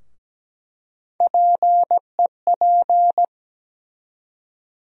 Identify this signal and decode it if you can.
Morse code, heard as a keyed tone, about 17 wpm, 700 Hz. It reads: PEP
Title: USA Call Sign Postfixes - 3 Letters 17wpm